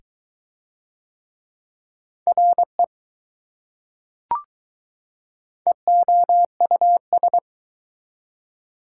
Morse code, heard as a keyed tone, about 20 wpm, 700 Hz. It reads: RE EOUS